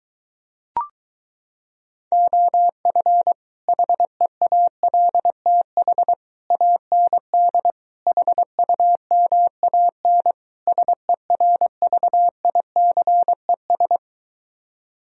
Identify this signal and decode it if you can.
Morse code, heard as a keyed tone, about 23 wpm, 700 Hz. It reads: OF HEALTH AND HUMAN SERVICES